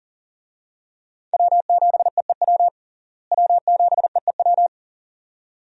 Morse code, heard as a keyed tone, about 40 wpm, 700 Hz. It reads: W7EEW W7EEW
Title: USA Call Signs - 5 Characters - Repeated 2x - 40wpm